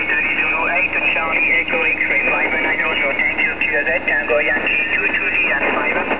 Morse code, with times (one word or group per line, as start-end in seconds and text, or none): none